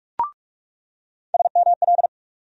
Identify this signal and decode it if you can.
Morse code, heard as a keyed tone, about 42 wpm, 700 Hz. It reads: SML
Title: USA Call Sign Postfixes - 3 Letters 45wpm